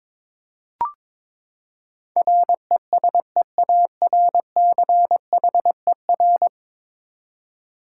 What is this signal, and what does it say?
Morse code, heard as a keyed tone, about 22 wpm, 700 Hz. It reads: RESEARCHER
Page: morse-code-ninja-practice-22wpm